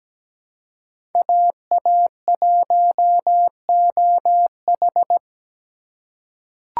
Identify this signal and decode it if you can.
Morse code, heard as a keyed tone, about 17 wpm, 700 Hz. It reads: AA1OH
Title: USA Call Signs - 8x Character Spacing - 17wpm